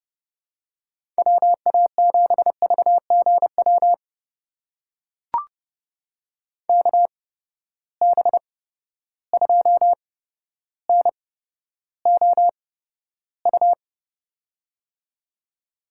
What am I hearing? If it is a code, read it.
Morse code, heard as a keyed tone, about 32 wpm, spaced as for 13 wpm, 700 Hz. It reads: WA7VGW KB2NOU
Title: USA Call Signs - 6 Characters - 8x Character Spacing - 30wpm